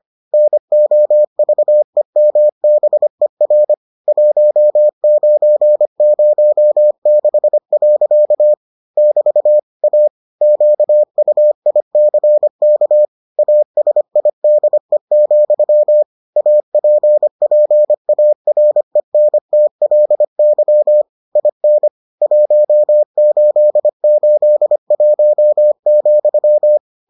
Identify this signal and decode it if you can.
Morse code, heard as a keyed tone, about 25 wpm, 600 Hz. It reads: NOVEMBER 1906. <BT> A QUICK ASIDE, APPARENTLY IN 1881,